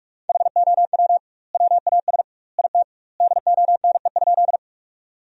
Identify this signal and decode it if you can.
Morse code, heard as a keyed tone, about 46 wpm, 700 Hz. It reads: HOW WAS IT DONE?
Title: 2x Word-Spacing - Sentences from Top 500 Words - 45wpm